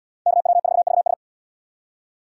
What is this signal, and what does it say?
Morse code, heard as a keyed tone, about 77 wpm, 700 Hz. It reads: KW4VS